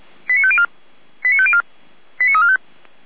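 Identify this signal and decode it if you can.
Five-tone selective call (ZVEI), heard as quick runs of five tones, tones of about 70 ms. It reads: 89504 89504 89345
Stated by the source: ZVEI five-tone selective call received by SDR.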